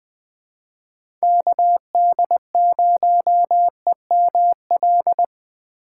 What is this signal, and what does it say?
Morse code, heard as a keyed tone, about 20 wpm, 700 Hz. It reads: KD0EML
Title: USA Call Signs - 6 Characters - 6x Character Spacing - 20wpm